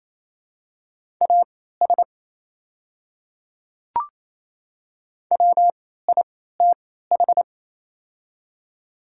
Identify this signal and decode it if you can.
Morse code, heard as a keyed tone, about 28 wpm, spaced as for 16 wpm, 690 Hz. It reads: AS WITH
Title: 3x Character Spacing - Top 100 Words - 28wpm